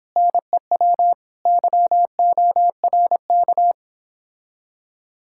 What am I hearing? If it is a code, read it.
Morse code, heard as a keyed tone, about 26 wpm, 700 Hz. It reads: NEW YORK